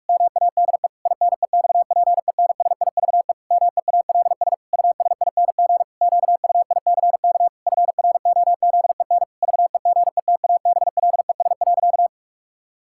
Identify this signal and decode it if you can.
Morse code, heard as a keyed tone, about 46 wpm, 700 Hz. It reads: MADE INEXPENSIVE MEALS USING QUICK FROZEN VEGETABLES.
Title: Pangram Challenge v2 - 45wpm